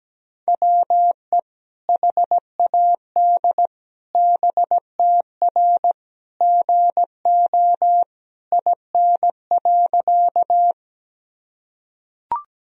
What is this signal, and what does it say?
Morse code, heard as a keyed tone, about 17 wpm, 700 Hz. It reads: WE HAD BTR GO IN.
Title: Sentences from Top 400 Words and Abbreviations 17wpm